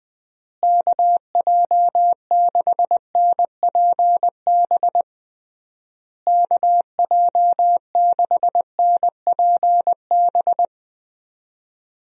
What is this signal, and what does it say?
Morse code, heard as a keyed tone, about 21 wpm, 700 Hz. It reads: KJ6NPB KJ6NPB
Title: USA Call Signs - 6 Characters - Repeated 3x - 20wpm